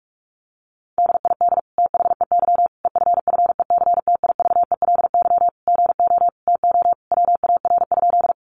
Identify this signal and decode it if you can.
Morse code, heard as a keyed tone, about 44 wpm, 700 Hz. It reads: DID THEY EFFECTIVELY GO TO WAR?
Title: Sentences from Top 1400 Words 45wpm